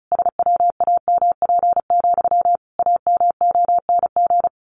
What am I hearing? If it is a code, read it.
Morse code, heard as a keyed tone, about 34 wpm, 700 Hz. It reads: SWAMP, AMONG
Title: A Sound Of Thunder 35wpm